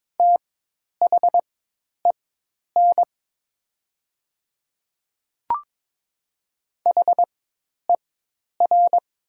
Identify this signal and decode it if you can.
Morse code, heard as a keyed tone, about 22 wpm, spaced as for 10 wpm, 700 Hz. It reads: THEN HER
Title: 4x Character Spacing - Top 100 Words - 22wpm